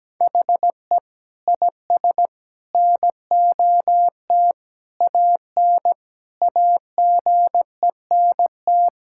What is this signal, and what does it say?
Morse code, heard as a keyed tone, about 17 wpm, 700 Hz. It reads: HE IS NOT AN AGENT
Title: Sentences from Top 1600 Words 17wpm